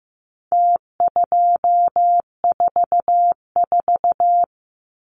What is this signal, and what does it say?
Morse code, heard as a keyed tone, about 15 wpm, 700 Hz. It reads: T244
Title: Numbers - Contest Style - 4-Digit 15wpm